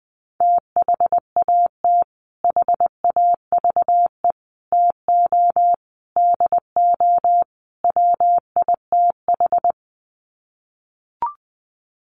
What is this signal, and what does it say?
Morse code, heard as a keyed tone, about 20 wpm, 700 Hz. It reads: THAT HAVE TO DO WITH
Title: Sets of 5 Words Encore2 20wpm